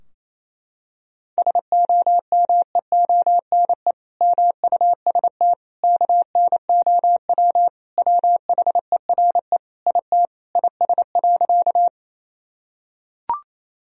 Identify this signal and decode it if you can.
Morse code, heard as a keyed tone, about 28 wpm, 700 Hz. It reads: SOMEONE MUST KNOW WHERE IT IS.
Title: Sentences from Top 1000 Words - Review Part 4 - 28wpm